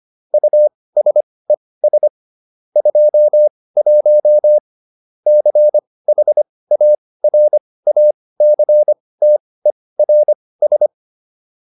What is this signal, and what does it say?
Morse code, heard as a keyed tone, about 25 wpm, spaced as for 18 wpm, 600 Hz. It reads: USES 21 CHARACTERS